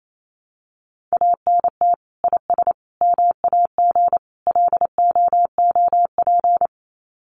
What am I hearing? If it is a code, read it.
Morse code, heard as a keyed tone, about 28 wpm, 700 Hz. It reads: ANT IS MAG LOOP
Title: Common QSO Phrases 28wpm